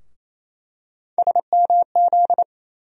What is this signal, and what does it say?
Morse code, heard as a keyed tone, about 28 wpm, 700 Hz. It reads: SMZ
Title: USA Call Sign Postfixes - 3 Letters 28wpm